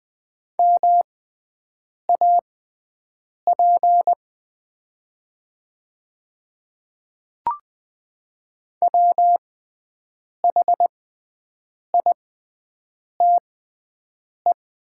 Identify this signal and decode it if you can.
Morse code, heard as a keyed tone, about 20 wpm, spaced as for 7 wpm, 700 Hz. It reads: MAP WHITE
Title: 6x Character Spacing - Top 500 Words - 20wpm